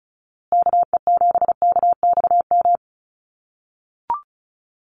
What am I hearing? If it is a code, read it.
Morse code, heard as a keyed tone, about 34 wpm, 700 Hz. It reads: KE7KXM
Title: USA Call Signs Repeated 2x 35wpm